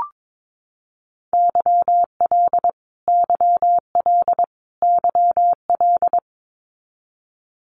Morse code, heard as a keyed tone, about 22 wpm, 700 Hz. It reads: YL YL YL
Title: Abbreviations Repeated 3x - 22wpm